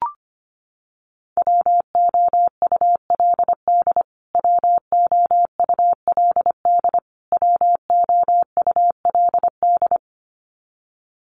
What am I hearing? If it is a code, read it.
Morse code, heard as a keyed tone, about 25 wpm, 700 Hz. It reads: WOULD WOULD WOULD
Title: Top 300 Words Repeated 25wpm